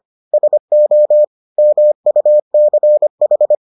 Morse code, heard as a keyed tone, about 25 wpm, 600 Hz. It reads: SO MUCH